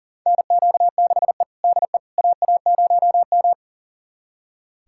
Morse code, heard as a keyed tone, about 40 wpm, 700 Hz. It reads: NQ6E DE AA0M
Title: USA Callsign DE Callsign - 4 Characters - 40wpm